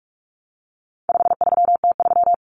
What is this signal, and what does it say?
Morse code, heard as a keyed tone, about 44 wpm, 700 Hz. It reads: 53T3
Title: Common Cut Numbers - 4-Digit 45wpm